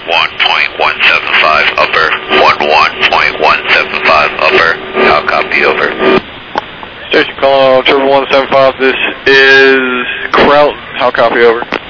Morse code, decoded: none

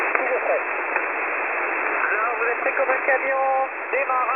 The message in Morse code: none

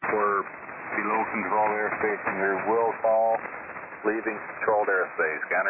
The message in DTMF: none

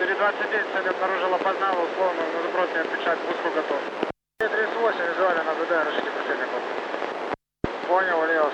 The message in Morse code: none

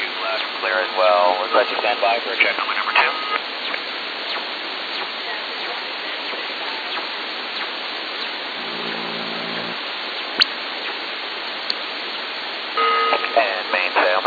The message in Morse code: none

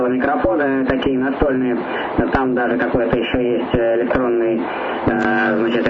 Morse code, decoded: none

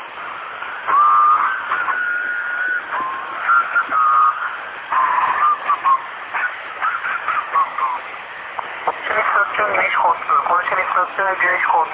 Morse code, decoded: none

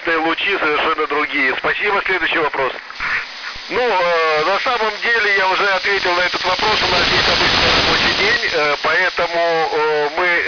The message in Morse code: none

